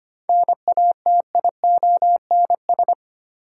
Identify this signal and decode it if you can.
Morse code, heard as a keyed tone, about 25 wpm, 700 Hz. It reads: NATIONS
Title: Sets of 3 Words - New Word Drill - 25wpm